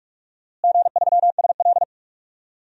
Morse code, heard as a keyed tone, about 46 wpm, 700 Hz. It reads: G3SL
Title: DXCC-100 Call Signs - 4 Characters - Repeated 2x - 45wpm